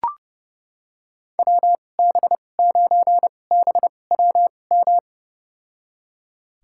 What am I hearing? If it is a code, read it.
Morse code, heard as a keyed tone, about 30 wpm, spaced as for 22 wpm, 700 Hz. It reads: WB9BWM
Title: USA Call Signs - 6 Characters - 2x Character Spacing - 30wpm